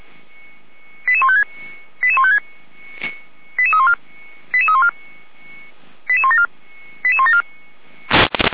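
Five-tone selective call (ZVEI), heard as quick runs of five tones, tones of about 70 ms. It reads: 88156 88156 88314 88314 88174 88174
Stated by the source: ZVEI five-tone selective call 88171 received with an SDR.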